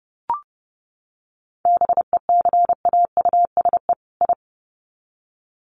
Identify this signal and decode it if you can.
Morse code, heard as a keyed tone, about 30 wpm, 700 Hz. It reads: BECAUSE I
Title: Sets of 2 Words 30wpm